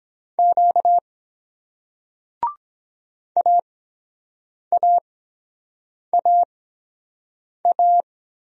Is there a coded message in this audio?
Yes: Morse code from 0.4 to 8.0 s, heard as a keyed tone, about 22 wpm, spaced as for 7 wpm, 700 Hz. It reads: Q AAAA